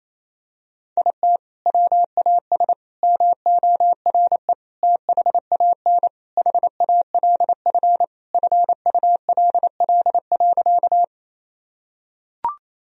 Morse code, heard as a keyed tone, about 28 wpm, 700 Hz. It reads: IT WAS MORE THAN HALF FULL.
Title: Sentences from Top 1000 Words - Review Part 3 - 28wpm